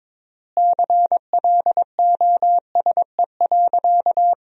Morse code, heard as a keyed tone, about 22 wpm, 700 Hz. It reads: CLOSE.
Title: 3x Word-Spacing - Sentences from Top 300 Words - 22wpm